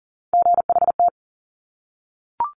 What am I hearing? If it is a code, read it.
Morse code, heard as a keyed tone, about 38 wpm, 700 Hz. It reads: GHT
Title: Top Three Letter Combinations in Words 40wpm